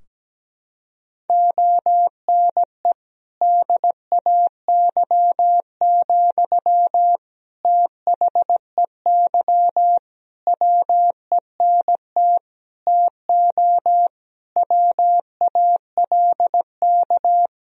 Morse code, heard as a keyed tone, about 17 wpm, 700 Hz. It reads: ONE DAY, THEY WENT TO WALK
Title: Sentences from Top 1000 Words - Review Part 2 - 17wpm